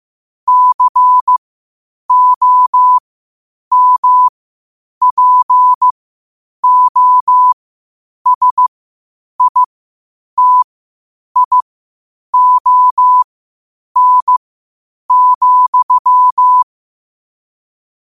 Morse code, heard as a keyed tone, about 15 wpm, spaced as for 8 wpm, 1000 Hz. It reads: COMPOSITION,
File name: qotdfiles / 8wpm